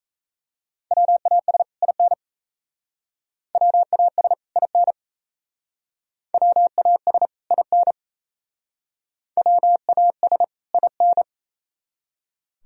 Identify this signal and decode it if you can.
Morse code, heard as a keyed tone, about 35 wpm, 700 Hz. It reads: WAS IN WAS IN WAS IN WAS IN
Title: Speed-Racing - Sets of 2 Words 28wpm